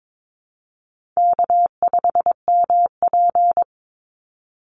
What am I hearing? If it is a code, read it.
Morse code, heard as a keyed tone, about 22 wpm, 700 Hz. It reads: K5MP
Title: Easy Call Signs Repeated 3x - 22wpm